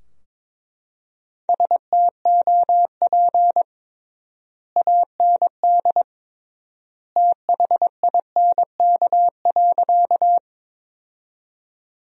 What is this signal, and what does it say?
Morse code, heard as a keyed tone, about 21 wpm, 700 Hz. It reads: STOP AND THINK.
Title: 3x Word-Spacing - Sentences from Top 300 Words - 22wpm